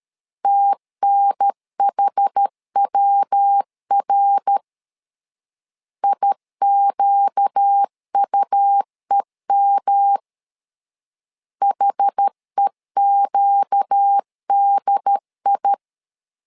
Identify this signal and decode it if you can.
Morse code, heard as a keyed tone, about 13 wpm, 800 Hz. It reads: TNHWR IQUEM HEQDI
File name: LEZIONI / 4 Q 12wpm 1833